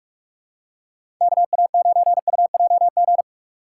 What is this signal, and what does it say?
Morse code, heard as a keyed tone, about 43 wpm, 700 Hz. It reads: KA9UJG
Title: USA Call Signs - 6 Characters - 4x Character Spacing - 45wpm